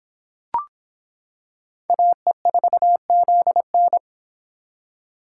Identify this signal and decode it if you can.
Morse code, heard as a keyed tone, about 26 wpm, 700 Hz. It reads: AE4ZN